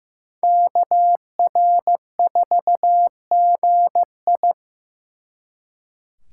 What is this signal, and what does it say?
Morse code, heard as a keyed tone, about 18 wpm, 700 Hz. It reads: KR4GI
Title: Speed Racing - USA 2x2 Call Signs 15wpm